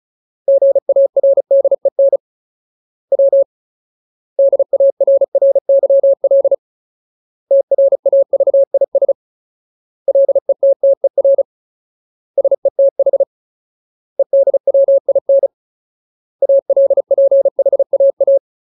Morse code, heard as a keyed tone, about 34 wpm, 550 Hz. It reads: GARDEN W DARRYL TRAVIS LETTER SETH EDWIN ALPHAA
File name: Ultimate Word List / 35WPM Ultimate